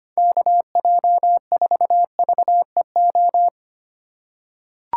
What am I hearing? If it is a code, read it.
Morse code, heard as a keyed tone, about 25 wpm, 700 Hz. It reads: KJ4VEO
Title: USA Call Signs - 3x Character Spacing - 25wpm